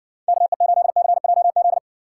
Morse code, heard as a keyed tone, about 62 wpm, 700 Hz. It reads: KE8CJZ